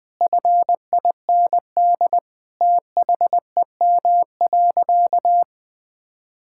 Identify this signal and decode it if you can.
Morse code, heard as a keyed tone, about 20 wpm, 700 Hz. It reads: FIND THEM.
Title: Sentences from Top 1000 Words - Review Part 5 - 20wpm